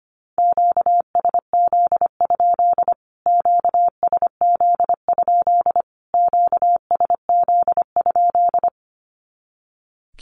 Morse code, heard as a keyed tone, about 25 wpm, 700 Hz. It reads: QSZ? QSZ? QSZ?
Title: Q-Codes Repeated 3x 25wpm